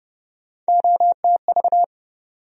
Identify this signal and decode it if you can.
Morse code, heard as a keyed tone, about 30 wpm, 700 Hz. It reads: OTV
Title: USA Call Sign Postfixes - 3 Letters 30wpm